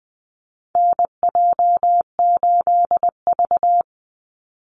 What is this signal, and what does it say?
Morse code, heard as a keyed tone, about 20 wpm, 700 Hz. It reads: NJ8V
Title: Call Signs Repeated 2x - 20wpm